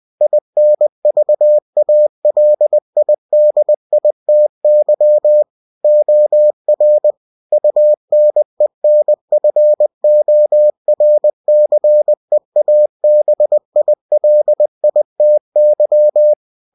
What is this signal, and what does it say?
Morse code, heard as a keyed tone, about 20 wpm, 600 Hz. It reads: INVALIDITY OR UNENFORCEABILITY
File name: cw-books / winnie-the-pooh / winnie-20wpm-page-0119